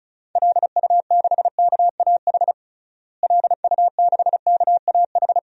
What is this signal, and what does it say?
Morse code, heard as a keyed tone, about 34 wpm, 700 Hz. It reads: LU6KAH LU6KAH
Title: DXCC-100 Call Signs - 6 Characters - Repeated 2x - 35wpm